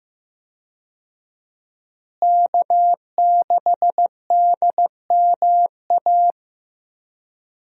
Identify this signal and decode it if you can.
Morse code, heard as a keyed tone, about 16 wpm, 700 Hz. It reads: K6DMA
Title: USA Call Signs - 5 Characters - Repeated 2x - 15wpm